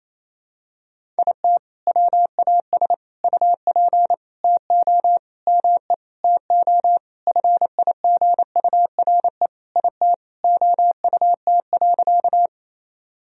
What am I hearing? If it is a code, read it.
Morse code, heard as a keyed tone, about 28 wpm, 700 Hz. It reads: IT WAS UP TO ME TO FIGURE IT OUT.
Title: Sentences from Top 1000 Words - Review Part 2 - 28wpm